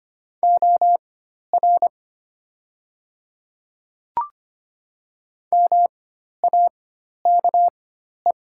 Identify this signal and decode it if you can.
Morse code, heard as a keyed tone, about 25 wpm, spaced as for 12 wpm, 700 Hz. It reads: OR MAKE
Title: 4x Character Spacing - Top 100 Words - 25wpm